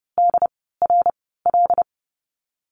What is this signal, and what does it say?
Morse code, heard as a keyed tone, about 30 wpm, 700 Hz. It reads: D R L
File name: Lesson 14g - Number 2 - Call Signs 30wpm